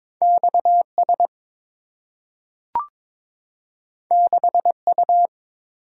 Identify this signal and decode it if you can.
Morse code, heard as a keyed tone, about 22 wpm, 690 Hz. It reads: XS 6U
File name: practice / 22wpm / Two Letter-Number Combinations - 22wpm